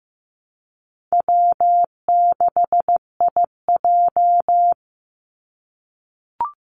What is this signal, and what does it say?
Morse code, heard as a keyed tone, about 15 wpm, 700 Hz. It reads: W6IJ
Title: Easy Call Signs - 15wpm